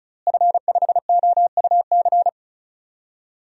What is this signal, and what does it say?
Morse code, heard as a keyed tone, about 36 wpm, 700 Hz. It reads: F5OUC
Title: DXCC-100 Call Signs - 5 Characters - Repeated 2x - 35wpm